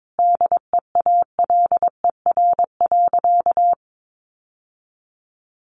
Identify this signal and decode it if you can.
Morse code, heard as a keyed tone, about 22 wpm, 700 Hz. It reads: DEALER.
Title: Sentences from Top 1100 Words 22wpm